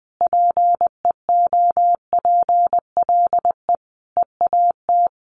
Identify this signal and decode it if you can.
Morse code, heard as a keyed tone, about 20 wpm, 700 Hz. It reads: PEOPLE EAT
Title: Sentences from Top 1300 Words 20wpm